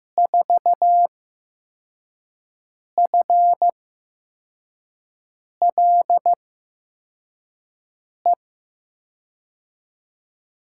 Morse code, heard as a keyed tone, about 15 wpm, spaced as for 4 wpm, 700 Hz. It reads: 4FLE
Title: USA Call Signs - 6 Characters - 8x Character Spacing - 15wpm